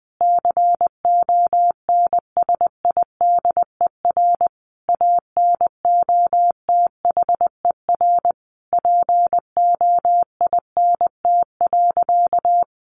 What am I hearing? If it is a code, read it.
Morse code, heard as a keyed tone, about 20 wpm, 700 Hz. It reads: CONSIDER ANOTHER POINT.
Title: Sentences from Top 600 Words 20wpm